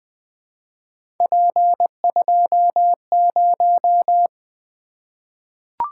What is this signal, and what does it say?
Morse code, heard as a keyed tone, about 20 wpm, 700 Hz. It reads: P20
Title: Three Letter-Number Combinations - 20wpm